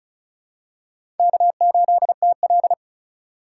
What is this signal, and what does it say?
Morse code, heard as a keyed tone, about 36 wpm, 700 Hz. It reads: K8TL
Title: USA Call Signs - 4 Characters - 2x Character Spacing - 35wpm